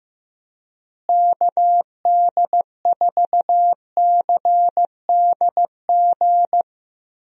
Morse code, heard as a keyed tone, about 15 wpm, 700 Hz. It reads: KD4CDG
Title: USA Call Signs - 6 Characters - Repeated 2x - 15wpm